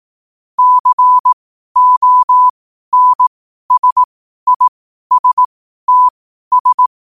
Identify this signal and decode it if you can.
Morse code, heard as a keyed tone, about 18 wpm, spaced as for 12 wpm, 1000 Hz. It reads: CONSISTS
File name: 12wpm